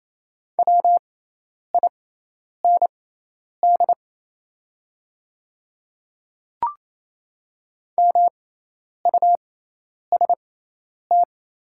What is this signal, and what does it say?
Morse code, heard as a keyed tone, about 28 wpm, spaced as for 10 wpm, 700 Hz. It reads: WIND MUST
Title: 6x Character Spacing - Top 500 Words - 28wpm